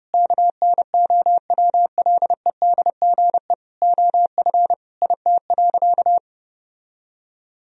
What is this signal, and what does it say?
Morse code, heard as a keyed tone, about 31 wpm, 700 Hz. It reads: KNOWLEDGE OF IT.
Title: Sentences from Top 1000 Words - Review Part 3 - 30wpm